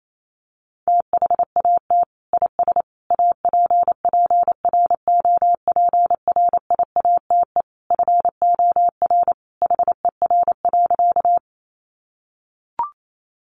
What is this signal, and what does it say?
Morse code, heard as a keyed tone, about 28 wpm, 690 Hz. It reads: THAT IS APPROPRIATE FOR HER.
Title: Sentences from Top 900 Words 28wpm